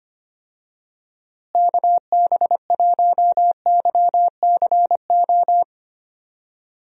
Morse code, heard as a keyed tone, about 25 wpm, 700 Hz. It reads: KB1YCO
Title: USA Call Signs - 6 Characters - Repeated 2x - 25wpm